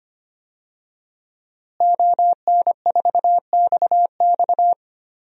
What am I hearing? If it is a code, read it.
Morse code, heard as a keyed tone, about 25 wpm, 700 Hz. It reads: ON4XX
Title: DXCC-100 Call Signs - 5 Characters - Repeated 2x - 25wpm